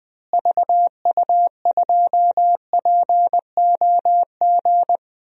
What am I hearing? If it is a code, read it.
Morse code, heard as a keyed tone, about 20 wpm, 700 Hz. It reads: VU2POG
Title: DXCC-100 Call Signs - 6 Characters - Repeated 3x - 20wpm